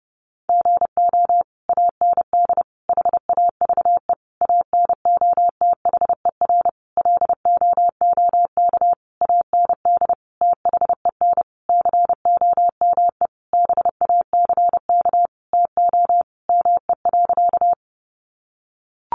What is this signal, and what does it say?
Morse code, heard as a keyed tone, about 30 wpm, 700 Hz. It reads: GO AND HAVE ANOTHER LOOK AND THEN COME BACK TO ME.
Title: Sentences from Top 600 Words 30wpm